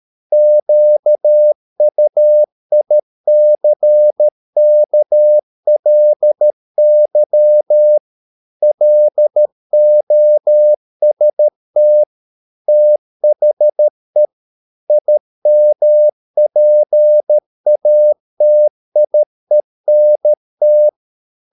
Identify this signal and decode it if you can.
Morse code, heard as a keyed tone, about 13 wpm, 600 Hz. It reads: QUICKLY LOST THE IMPATIENT
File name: Never 0013wpm